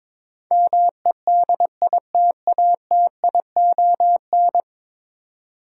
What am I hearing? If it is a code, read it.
Morse code, heard as a keyed tone, about 22 wpm, 700 Hz. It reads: MEDITATION
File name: practice / 22wpm / 10 Letter Words - First 3 Letters Given - 22wpm